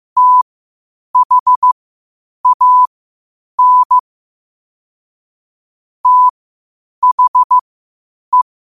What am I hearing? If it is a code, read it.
Morse code, heard as a keyed tone, about 15 wpm, spaced as for 8 wpm, 1000 Hz. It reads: THAN THE
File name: qotdfiles / 8wpm